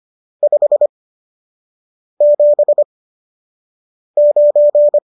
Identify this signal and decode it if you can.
Morse code, heard as a keyed tone, about 25 wpm, spaced as for 6 wpm, 590 Hz. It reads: 579